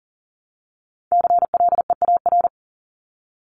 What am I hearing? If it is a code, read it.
Morse code, heard as a keyed tone, about 40 wpm, 700 Hz. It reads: CLEAR
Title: LY Suffix - Top 100 Words with Base Words 40wpm